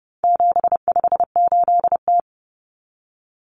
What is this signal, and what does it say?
Morse code, heard as a keyed tone, about 30 wpm, 700 Hz. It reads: 758T
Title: Common Cut Numbers - 4-Digit 30wpm